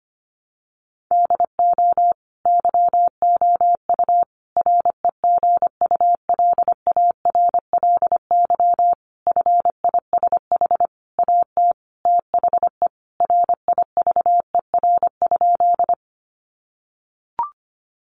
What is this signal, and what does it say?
Morse code, heard as a keyed tone, about 25 wpm, 700 Hz. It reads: DO YOU REGULARLY FISH AT THE RIVER?
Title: Sentences from Top 1400 Words 25wpm